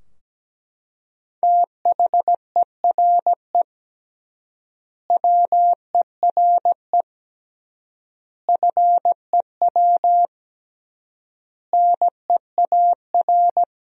Morse code, heard as a keyed tone, about 17 wpm, 700 Hz. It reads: THERE WERE FEW NEAR
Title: 3x Word-Spacing - Sentences from Top 300 Words - 17wpm